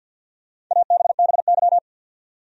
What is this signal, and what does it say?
Morse code, heard as a keyed tone, about 52 wpm, 700 Hz. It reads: ABBY
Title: USA Common Names - Max 5 Letters - 50wpm